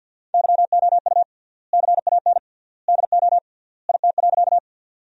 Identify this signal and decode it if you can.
Morse code, heard as a keyed tone, about 50 wpm, 700 Hz. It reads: YOU CAN DO IT.
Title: 3x Word-Spacing - Sentences from Top 100 Words - 50wpm